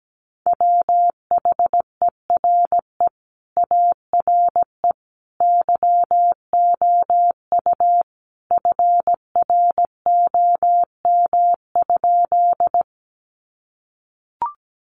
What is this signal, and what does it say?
Morse code, heard as a keyed tone, about 17 wpm, 700 Hz. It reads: WHERE ARE YOU FROM?
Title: Sentences from Top 200 Words 17wpm